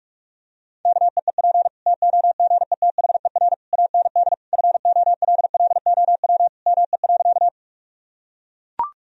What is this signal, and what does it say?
Morse code, heard as a keyed tone, about 43 wpm, 700 Hz. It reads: KEEP TOGETHER AND FOLLOW ME.
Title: Sentences from Top 1000 Words - Review Part 1 - 45wpm